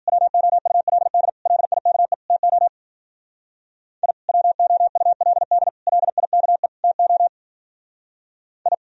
Morse code, heard as a keyed tone, about 47 wpm, 700 Hz. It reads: WOULD LIKE TO I WOULD LIKE TO I
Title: Speed-Racing - Sets of 4 Words 40wpm